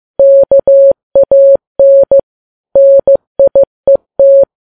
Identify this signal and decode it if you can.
Morse code, heard as a keyed tone, about 15 wpm, 550 Hz. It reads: KAN NIET
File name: Files Web tls / Morse on9fv / morse15kps